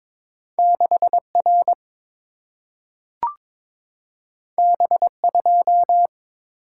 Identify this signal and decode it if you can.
Morse code, heard as a keyed tone, about 22 wpm, 700 Hz. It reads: 6 R B 2
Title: Single-Letter Single-Number 22wpm